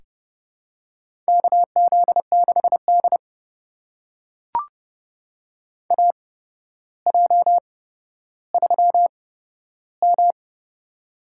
Morse code, heard as a keyed tone, about 30 wpm, spaced as for 13 wpm, 700 Hz. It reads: KZ6D AJ3M
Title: USA Call Signs - 4 Characters - 8x Character Spacing - 30wpm